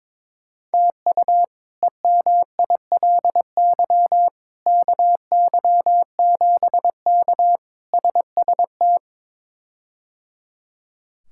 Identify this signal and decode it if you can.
Morse code, heard as a keyed tone, about 21 wpm, 700 Hz. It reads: TU EMILY KY7K SST
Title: K1USN SST QSO Practice - 22wpm